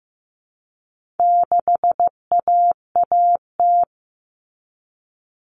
Morse code, heard as a keyed tone, about 15 wpm, 700 Hz. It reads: 6AAT
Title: Common Cut Numbers - 4-Digit 15wpm